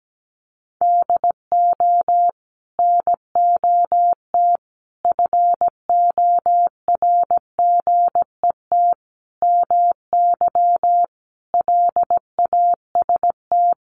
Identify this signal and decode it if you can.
Morse code, heard as a keyed tone, about 17 wpm, 700 Hz. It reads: DO NOT FORGET MY LAST
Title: Sentences from Top 1200 Words 17wpm